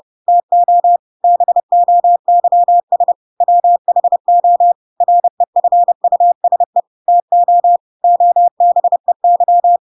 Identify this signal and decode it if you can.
Morse code, heard as a keyed tone, about 30 wpm, 700 Hz. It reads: TO BOYS WHO REFUSE TO OBEY
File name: Pinocchio / Pinocchio ch4 30 30 ik1hge